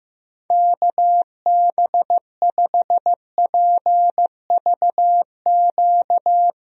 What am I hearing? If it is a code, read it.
Morse code, heard as a keyed tone, about 15 wpm, 700 Hz. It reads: KB5PVQ
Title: USA Call Signs - 6 Characters - Repeated 3x - 15wpm